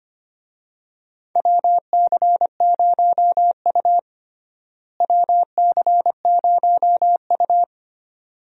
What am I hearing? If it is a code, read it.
Morse code, heard as a keyed tone, about 25 wpm, 700 Hz. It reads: WC0U WC0U
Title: USA Call Signs - 4 Characters - Repeated 2x - 25wpm